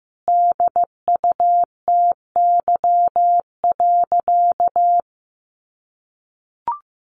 Morse code, heard as a keyed tone, about 15 wpm, 700 Hz. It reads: DUTY.
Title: Sentences from Top 1500 Words 15wpm